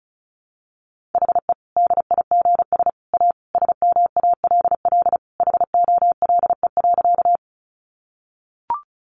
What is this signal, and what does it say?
Morse code, heard as a keyed tone, about 34 wpm, 700 Hz. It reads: HE DIGS A SMALL HOLE.
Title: Easy Phrases 35wpm